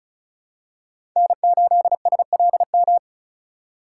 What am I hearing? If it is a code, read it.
Morse code, heard as a keyed tone, about 34 wpm, 700 Hz. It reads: N8SLM
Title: USA Call Signs - 5 Characters - 4x Character Spacing - 35wpm